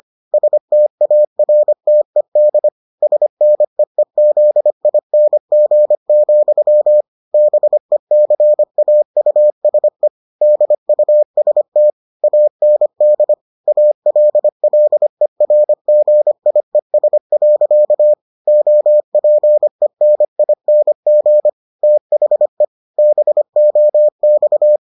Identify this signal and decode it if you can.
Morse code, heard as a keyed tone, about 25 wpm, 600 Hz. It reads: STARTED SNEEZING, BECAUSE DUST AND ALLERGIES. OPENING THE BOX